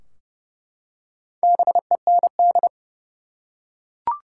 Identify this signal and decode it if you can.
Morse code, heard as a keyed tone, about 29 wpm, 700 Hz. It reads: BEND
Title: Four Letter Words - 30wpm